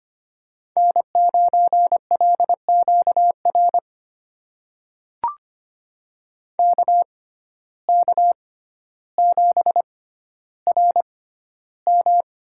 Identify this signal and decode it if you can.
Morse code, heard as a keyed tone, about 25 wpm, spaced as for 13 wpm, 690 Hz. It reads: N9LQR KK7RM
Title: USA Call Signs - 5 Characters - 6x Character Spacing - 25wpm